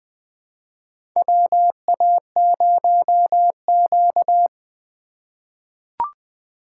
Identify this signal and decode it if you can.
Morse code, heard as a keyed tone, about 20 wpm, 700 Hz. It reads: WA0Q
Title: USA Call Signs - 2x Character Spacing - 20wpm